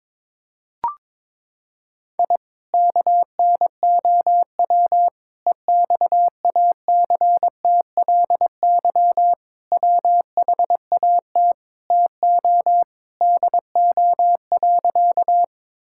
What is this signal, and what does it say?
Morse code, heard as a keyed tone, about 22 wpm, 700 Hz. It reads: I KNOW EXACTLY WHAT TO DO.
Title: Sentences from Top 1000 Words - Review Part 6 - 22wpm